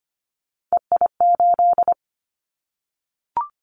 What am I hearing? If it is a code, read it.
Morse code, heard as a keyed tone, about 25 wpm, 700 Hz. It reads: EI8
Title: Callsign Prefixes for Top 50 DXCC - 25wpm